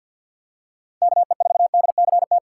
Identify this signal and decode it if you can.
Morse code, heard as a keyed tone, about 47 wpm, 700 Hz. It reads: KE4DCT
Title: USA Call Signs - 6 Characters - 3x Character Spacing - 50wpm